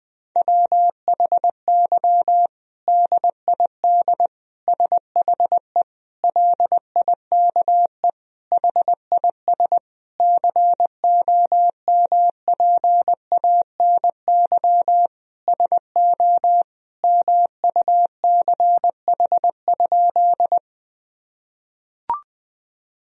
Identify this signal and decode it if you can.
Morse code, heard as a keyed tone, about 20 wpm, 700 Hz. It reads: WHY DID SHE LIKE HIS COMPANY SO MUCH?
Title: Sentences from Top 1000 Words - Review Part 3 - 20wpm